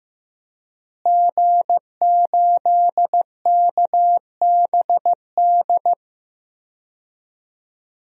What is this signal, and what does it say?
Morse code, heard as a keyed tone, about 15 wpm, 700 Hz. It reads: G8KBD
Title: DXCC-100 Call Signs - 5 Characters - Repeated 2x - 15wpm